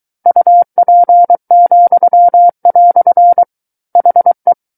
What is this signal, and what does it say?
Morse code, heard as a keyed tone, about 23 wpm, 700 Hz. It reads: UP," HE
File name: Book One / 14-Chapter 14 - 23 WPM